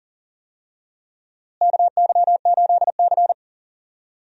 Morse code, heard as a keyed tone, about 40 wpm, 700 Hz. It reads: KY8C
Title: USA Call Signs - 4 Characters - Repeated 3x - 40wpm